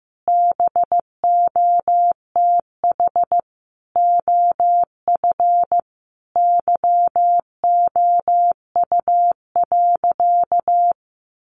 Morse code, heard as a keyed tone, about 15 wpm, 700 Hz. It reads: BOTH OF YOU.
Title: Sentences from Top 1100 Words 15wpm